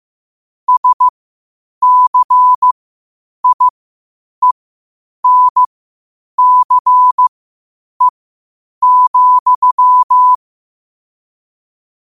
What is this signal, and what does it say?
Morse code, heard as a keyed tone, about 15 wpm, spaced as for 8 wpm, 1000 Hz. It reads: SCIENCE,
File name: qotdfiles / 8wpm